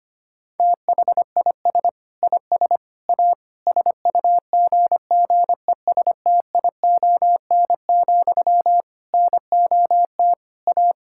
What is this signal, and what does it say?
Morse code, heard as a keyed tone, about 25 wpm, 700 Hz. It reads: THIS IS A SUGGESTION, NOT A
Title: Sentences from Top 1600 Words 25wpm